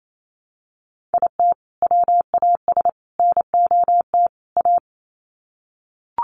Morse code, heard as a keyed tone, about 28 wpm, 700 Hz. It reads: IT WAS NOT A
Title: Sets of 4 Words 28wpm